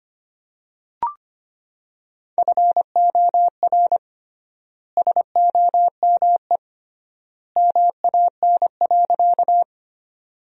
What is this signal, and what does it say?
Morse code, heard as a keyed tone, about 25 wpm, 700 Hz. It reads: FOR SOME MAN.
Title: 3x Word-Spacing - Sentences from Top 200 Words - 25wpm